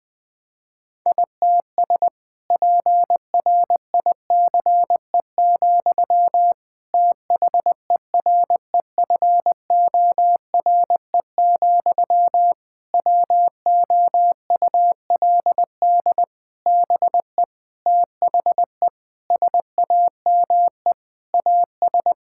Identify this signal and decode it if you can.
Morse code, heard as a keyed tone, about 19 wpm, 700 Hz. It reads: ITS PRICE, THEREFORE, WOULD BE THE SAME AS
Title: Sentences from Top 1000 Words - Review Part 5 - 20wpm